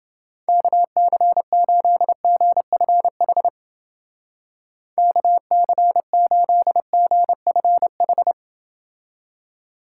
Morse code, heard as a keyed tone, about 29 wpm, 700 Hz. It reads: KC8GFH KC8GFH